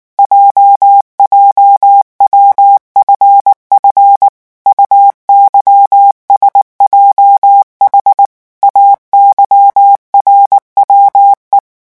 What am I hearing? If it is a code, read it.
Morse code, heard as a keyed tone, about 19 wpm, 800 Hz. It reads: JJWFF UYSJH AYRWE
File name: morse mp3 / revision7-vitesse 18 mots